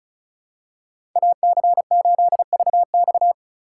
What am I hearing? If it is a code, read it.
Morse code, heard as a keyed tone, about 34 wpm, 700 Hz. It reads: AC8VX
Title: USA Call Signs - 5 Characters - 4x Character Spacing - 35wpm